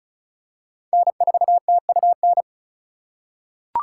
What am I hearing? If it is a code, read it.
Morse code, heard as a keyed tone, about 36 wpm, 700 Hz. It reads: N4TUN
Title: USA Call Signs - 5 Characters - 4x Character Spacing - 35wpm